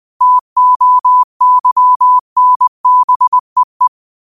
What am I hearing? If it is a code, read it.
Morse code, heard as a keyed tone, about 20 wpm, 1000 Hz. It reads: TOYNBEE